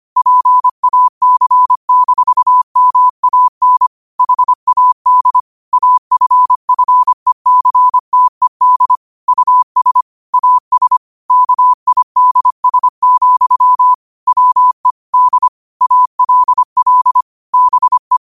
Morse code, heard as a keyed tone, about 25 wpm, 1000 Hz. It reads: PAC-MAN HAD AFFECTED US AS KIDS, WE D ALL BE
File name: qotdfiles / 25wpm